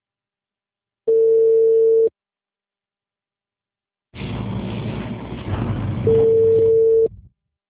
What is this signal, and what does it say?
Telephone call progress tone (China, 450 Hz), heard as a telephone network's tone, about 450 Hz. Ringback tone.